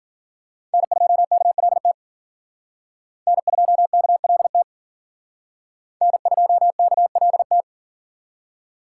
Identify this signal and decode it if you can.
Morse code, heard as a keyed tone, about 47 wpm, 700 Hz. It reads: N2KLT N2KLT N2KLT
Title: Speed Racing - USA Call Signs 40wpm